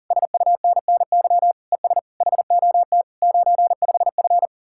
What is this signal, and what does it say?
Morse code, heard as a keyed tone, about 49 wpm, 700 Hz. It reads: SUNNY ES HOT 95F
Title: Speed-Racing Common QSO Phrases - 40wpm